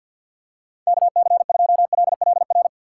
Morse code, heard as a keyed tone, about 50 wpm, 700 Hz. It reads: KC2LLR
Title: USA Call Signs - 6 Characters - 6x Character Spacing - 50wpm